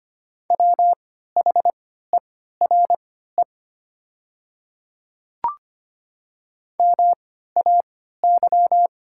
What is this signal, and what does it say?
Morse code, heard as a keyed tone, about 25 wpm, spaced as for 14 wpm, 700 Hz. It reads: WHERE MAY
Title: 3x Character Spacing - Top 200 Words - 25wpm